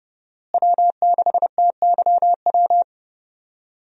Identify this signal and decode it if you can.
Morse code, heard as a keyed tone, about 30 wpm, 700 Hz. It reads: W6TYW
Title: USA Call Signs - 5 Characters - Repeated 2x - 30wpm